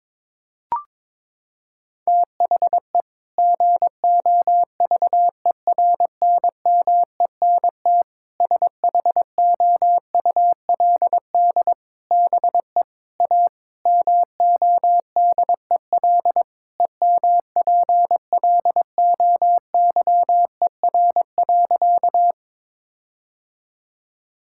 Morse code, heard as a keyed tone, about 22 wpm, 700 Hz. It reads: THE GOVERNMENT SHOULD BE A MODEL EMPLOYER.
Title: Sentences from Top 1600 Words 22wpm